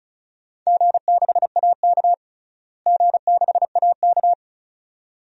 Morse code, heard as a keyed tone, about 34 wpm, 700 Hz. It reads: G6AK G6AK
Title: DXCC-100 Call Signs - 4 Characters - Repeated 2x - 35wpm